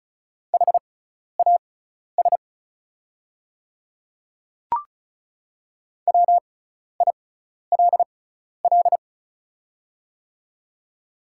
Morse code, heard as a keyed tone, about 36 wpm, spaced as for 12 wpm, 700 Hz. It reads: HAS WILL
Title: 6x Character Spacing - Top 100 Words - 35wpm